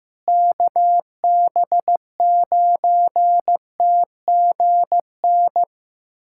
Morse code, heard as a keyed tone, about 15 wpm, 700 Hz. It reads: KB9TGN
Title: USA Callsign DE Callsign - 6 Characters - 15wpm